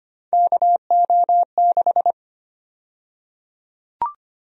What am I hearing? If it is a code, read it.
Morse code, heard as a keyed tone, about 25 wpm, 700 Hz. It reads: KO6
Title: USA Call Sign Prefixes - Repeated 2x - 25wpm